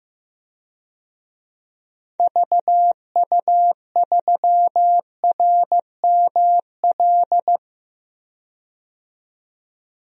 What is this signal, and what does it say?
Morse code, heard as a keyed tone, about 15 wpm, 700 Hz. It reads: VU3RML
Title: DXCC-100 Call Signs - 6 Characters - 15wpm